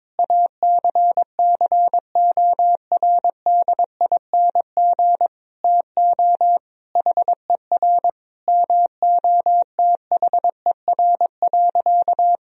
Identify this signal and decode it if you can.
Morse code, heard as a keyed tone, about 22 wpm, 700 Hz. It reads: ACCORDING TO HER MOTHER.
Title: Sentences from Top 1000 Words - Review Part 4 - 22wpm